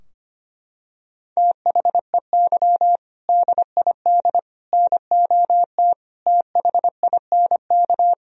Morse code, heard as a keyed tone, about 25 wpm, 700 Hz. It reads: THEY DID NOT THINK
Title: Sentences from Top 1000 Words - Review Part 3 - 25wpm